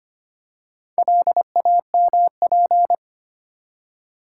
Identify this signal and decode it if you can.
Morse code, heard as a keyed tone, about 25 wpm, 700 Hz. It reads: LAMP
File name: practice / 25wpm / Easy Words with Hints - 25wpm